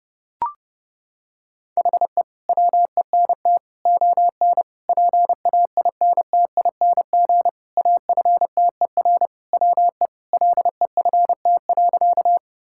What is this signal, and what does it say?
Morse code, heard as a keyed tone, about 30 wpm, 700 Hz. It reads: HE WENT ON PAINTING AFTER WE LEFT.
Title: Sentences from Top 1000 Words - Review Part 6 - 30wpm